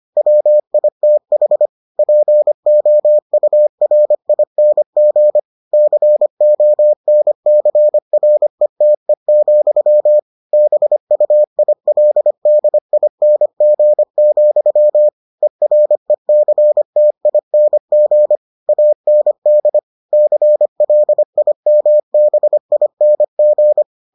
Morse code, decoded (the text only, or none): WITH POURING CONCRETE, BUILDING, ERECTING AND CLIMBING